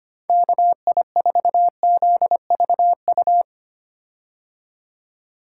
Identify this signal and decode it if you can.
Morse code, heard as a keyed tone, about 25 wpm, 700 Hz. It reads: KI4ZVU
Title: K1USN SST QSO Practice - 25wpm